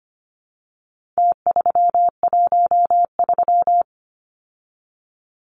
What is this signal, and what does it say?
Morse code, heard as a keyed tone, about 25 wpm, 700 Hz. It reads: T313
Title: Numbers - Contest Style - 4-Digit 25wpm